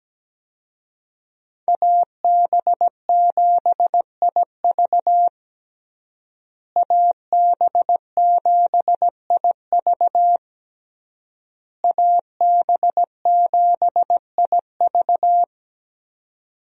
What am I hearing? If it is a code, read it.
Morse code, heard as a keyed tone, about 17 wpm, 700 Hz. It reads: AB7IV AB7IV AB7IV
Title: USA Call Signs - 5 Characters - Repeated 3x - 17wpm